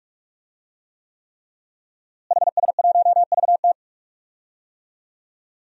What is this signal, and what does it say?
Morse code, heard as a keyed tone, about 46 wpm, 700 Hz. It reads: HS1VT
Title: DXCC-100 Call Signs - 5 Characters - Repeated 3x - 45wpm